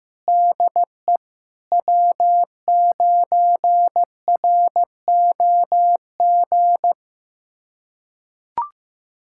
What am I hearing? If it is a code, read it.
Morse code, heard as a keyed tone, about 16 wpm, 700 Hz. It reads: DE W9ROG
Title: USA Callsign DE Callsign - 5 Characters - 15wpm